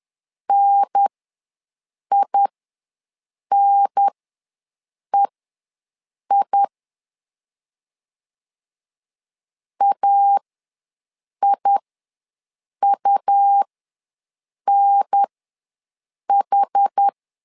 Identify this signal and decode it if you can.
Morse code, heard as a keyed tone, about 10 wpm, spaced as for 6 wpm, 800 Hz. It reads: NINEI AIUNH
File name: 2 UHDN ws 10wpm